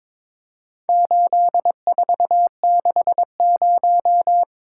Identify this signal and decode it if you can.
Morse code, heard as a keyed tone, about 22 wpm, 700 Hz. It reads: 8460
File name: practice / 22wpm / Four Letter-Number Combinations - 22wpm